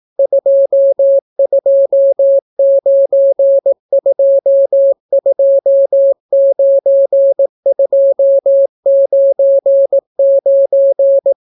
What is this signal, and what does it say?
Morse code, heard as a keyed tone, about 18 wpm, 550 Hz. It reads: 229229299